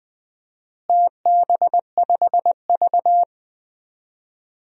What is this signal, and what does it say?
Morse code, heard as a keyed tone, about 20 wpm, 700 Hz. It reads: TB5V
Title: DXCC-100 Call Signs - 4 Characters - Repeated 2x - 20wpm